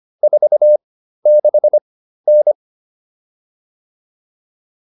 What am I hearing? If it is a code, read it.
Morse code, heard as a keyed tone, about 25 wpm, spaced as for 13 wpm, 600 Hz. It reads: 46N